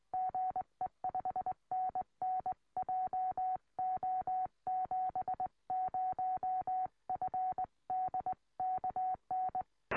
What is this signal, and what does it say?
Morse code, heard as a keyed tone, about 18 wpm, 740 Hz. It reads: GE 5NN JO70FD KN
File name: tropo / g3xdy 70cm